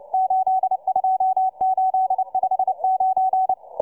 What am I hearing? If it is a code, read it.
Morse code, heard as a keyed tone, about 29 wpm, 750 Hz, with radio noise. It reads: 82859